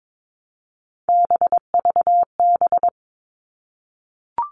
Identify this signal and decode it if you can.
Morse code, heard as a keyed tone, about 22 wpm, 700 Hz. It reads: BVB
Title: All Cut Numbers - 3-Digit 22wpm